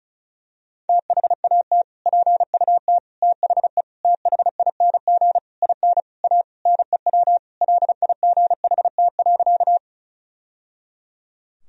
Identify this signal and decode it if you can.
Morse code, heard as a keyed tone, about 36 wpm, 700 Hz. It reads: THAT PUT THE THING IN A NEW LIGHT.
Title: Sentences from Top 1000 Words - Review Part 1 - 35wpm